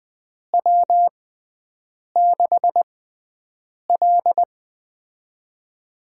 Morse code, heard as a keyed tone, about 20 wpm, spaced as for 7 wpm, 700 Hz. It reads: W6L
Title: USA Call Signs - 4 Characters - 6x Character Spacing - 20wpm